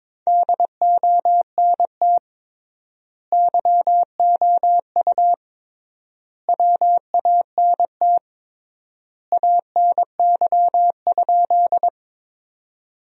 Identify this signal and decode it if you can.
Morse code, heard as a keyed tone, about 22 wpm, 700 Hz. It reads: DONT YOU WANT ANY?
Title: 3x Word-Spacing - Sentences from Top 300 Words - 22wpm